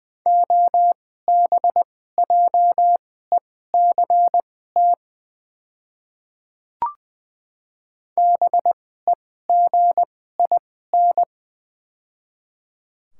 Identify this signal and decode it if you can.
Morse code, heard as a keyed tone, about 21 wpm, spaced as for 15 wpm, 700 Hz. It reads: OBJECT BEGIN
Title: 2x Character Spacing - Top 500 Words - 20wpm